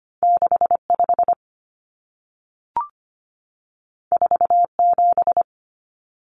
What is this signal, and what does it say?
Morse code, heard as a keyed tone, about 25 wpm, 700 Hz. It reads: 65 47
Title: Numbers - 2-Digit 25wpm